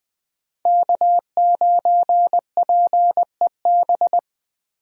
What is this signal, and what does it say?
Morse code, heard as a keyed tone, about 20 wpm, 700 Hz. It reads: K9PEB
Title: USA Call Signs - 5 Characters - 20wpm